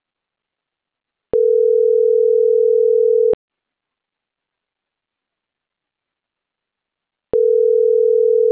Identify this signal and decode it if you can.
Ringback tone (North America), 440 + 480 Hz.